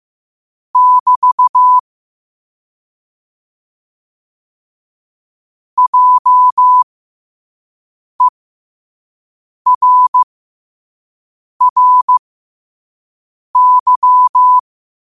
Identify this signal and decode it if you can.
Morse code, heard as a keyed tone, about 15 wpm, spaced as for 5 wpm, 1000 Hz. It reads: <BT> JERRY
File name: qotdfiles / 5wpm